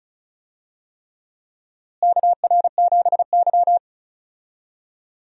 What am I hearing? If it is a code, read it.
Morse code, heard as a keyed tone, about 36 wpm, 700 Hz. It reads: KR7Y
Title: USA Call Signs - 4 Characters - Repeated 3x - 35wpm